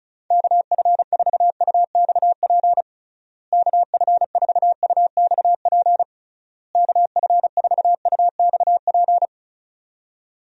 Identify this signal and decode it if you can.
Morse code, heard as a keyed tone, about 34 wpm, 700 Hz. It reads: KF4UXP KF4UXP KF4UXP
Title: USA Call Signs - 6 Characters - Repeated 3x - 35wpm